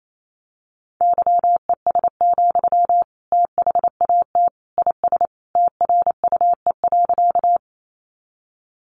Morse code, heard as a keyed tone, about 28 wpm, 700 Hz. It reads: YES, THAT IS TRUE.
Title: Sentences from Top 500 Words 28wpm